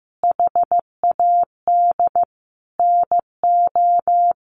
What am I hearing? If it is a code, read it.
Morse code, heard as a keyed tone, about 15 wpm, 700 Hz. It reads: HAD NO
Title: Sentences from Top 1200 Words 15wpm